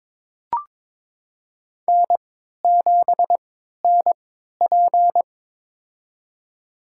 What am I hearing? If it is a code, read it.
Morse code, heard as a keyed tone, about 22 wpm, spaced as for 12 wpm, 700 Hz. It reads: N7NP
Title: USA Call Signs - 4 Characters - 3x Character Spacing - 22wpm